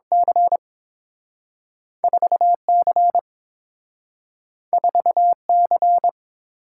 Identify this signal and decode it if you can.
Morse code, heard as a keyed tone, about 30 wpm, 700 Hz. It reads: C 4C 4C